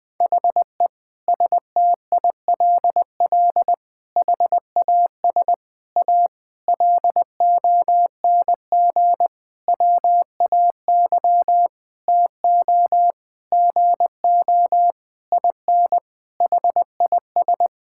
Morse code, heard as a keyed tone, about 20 wpm, 700 Hz. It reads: HE STILL HAS A LONG WAY TO GO IN HIS
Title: Sentences with 7-Word Phrases - 20wpm